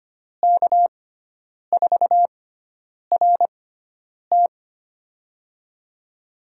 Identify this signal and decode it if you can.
Morse code, heard as a keyed tone, about 25 wpm, spaced as for 9 wpm, 700 Hz. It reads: K4RT
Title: USA Call Signs - 5 Characters - 6x Character Spacing - 25wpm